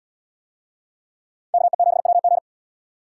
Morse code, heard as a keyed tone, about 71 wpm, 700 Hz. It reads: KE7WG